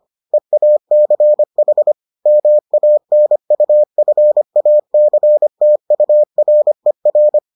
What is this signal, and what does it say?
Morse code, heard as a keyed tone, about 25 wpm, 600 Hz. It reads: EACH MANUFACTURER